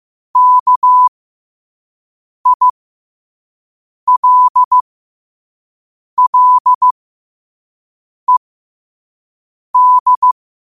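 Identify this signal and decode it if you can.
Morse code, heard as a keyed tone, about 15 wpm, spaced as for 5 wpm, 1000 Hz. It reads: KILLED
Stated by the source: Morse code practice at 5 wpm.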